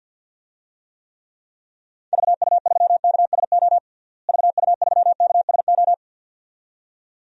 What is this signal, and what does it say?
Morse code, heard as a keyed tone, about 50 wpm, 700 Hz. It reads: VU3KSO VU3KSO
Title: DXCC-100 Call Signs - 6 Characters - Repeated 2x - 50wpm